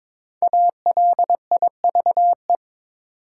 Morse code, heard as a keyed tone, about 22 wpm, 690 Hz. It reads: ALIVE
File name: 5 Letter Words - First 3 Letters Given - 22wpm